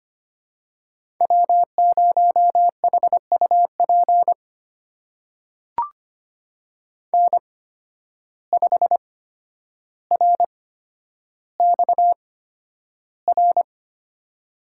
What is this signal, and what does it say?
Morse code, heard as a keyed tone, about 25 wpm, spaced as for 11 wpm, 700 Hz. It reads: W0HUP N5RXR
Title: USA Call Signs - 5 Characters - 8x Character Spacing - 25wpm